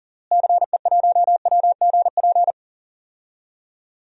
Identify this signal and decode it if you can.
Morse code, heard as a keyed tone, about 40 wpm, 700 Hz. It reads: CE1WGP
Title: DXCC-100 Call Signs - 6 Characters - Repeated 3x - 40wpm